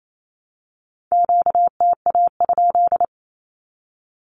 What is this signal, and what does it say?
Morse code, heard as a keyed tone, about 28 wpm, 700 Hz. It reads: QTA?
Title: Q-Codes Repeated 3x 28wpm